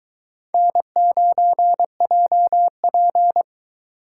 Morse code, heard as a keyed tone, about 21 wpm, 700 Hz. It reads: N9JP